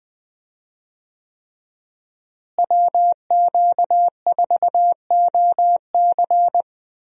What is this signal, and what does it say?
Morse code, heard as a keyed tone, about 20 wpm, 700 Hz. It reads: WQ4OC
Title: USA Call Signs - 5 Characters - Repeated 3x - 20wpm